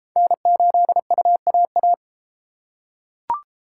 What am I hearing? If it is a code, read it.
Morse code, heard as a keyed tone, about 31 wpm, 700 Hz. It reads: N8UAA